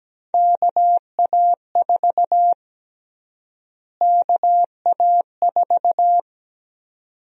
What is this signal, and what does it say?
Morse code, heard as a keyed tone, about 17 wpm, 700 Hz. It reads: KA4 KA4
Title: USA Call Sign Prefixes - Repeated 2x - 17wpm